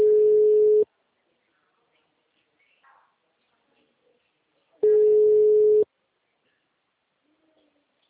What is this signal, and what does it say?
Ringback tone (Europe, 425 Hz).